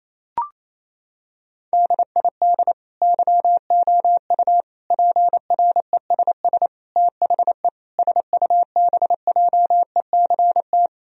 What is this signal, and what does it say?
Morse code, heard as a keyed tone, about 28 wpm, 700 Hz. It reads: DID YOU PRESS THE SUBJECT
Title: Sentences from Top 1000 Words - Review Part 1 - 28wpm